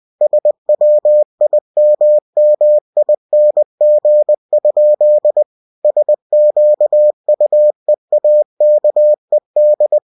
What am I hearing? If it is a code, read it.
Morse code, heard as a keyed tone, about 20 wpm, 600 Hz. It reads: SWIMMING? SQUEAKED
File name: cw-books / winnie-the-pooh / winnie-20wpm-page-0087